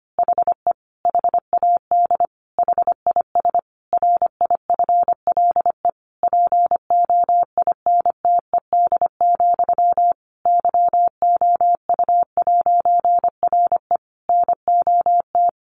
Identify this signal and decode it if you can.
Morse code, heard as a keyed tone, about 25 wpm, 700 Hz. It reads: HE HAD HIS RIFLE POINTED, YOU'RE NOT
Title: A Sound Of Thunder 25wpm